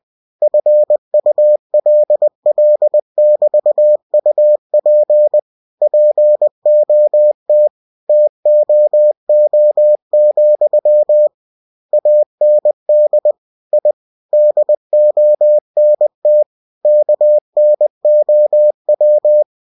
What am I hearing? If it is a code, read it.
Morse code, heard as a keyed tone, about 20 wpm, 600 Hz. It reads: FULL<BT>UP POT TOO, AND I DONT KNOW
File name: cw-books / winnie-the-pooh / winnie-20wpm-page-0039